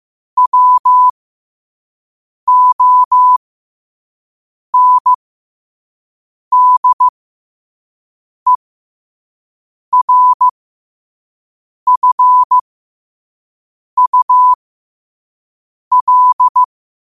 Morse code, heard as a keyed tone, about 15 wpm, spaced as for 5 wpm, 1000 Hz. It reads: WONDERFUL
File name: qotdfiles / 5wpm